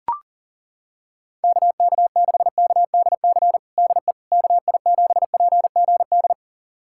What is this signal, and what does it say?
Morse code, heard as a keyed tone, about 40 wpm, 700 Hz. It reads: KK6KDC DE KI7PGD
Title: USA Callsign DE Callsign 40wpm